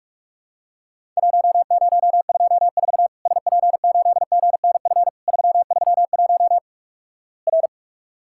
Morse code, heard as a keyed tone, about 46 wpm, 700 Hz. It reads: 1024 SP8GNF 331 R
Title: Worked All Europe DX Contest - QTC Exchange - 45wpm